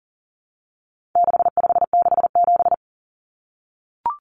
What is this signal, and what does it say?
Morse code, heard as a keyed tone, about 40 wpm, 700 Hz. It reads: 6567
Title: Numbers - 4-Digit 40wpm